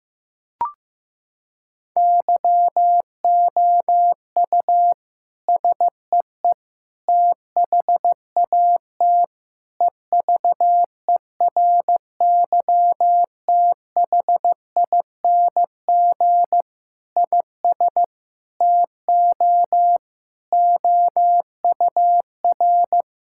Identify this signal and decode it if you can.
Morse code, heard as a keyed tone, about 15 wpm, 700 Hz. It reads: YOU SEE THAT EVERYTHING IS TO OUR
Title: Sentences from Top 1000 Words - Review Part 6 - 15wpm